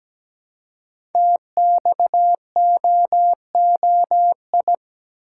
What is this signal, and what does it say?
Morse code, heard as a keyed tone, about 17 wpm, 700 Hz. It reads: TXOOI
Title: Code Groups - 5 Letters - 17wpm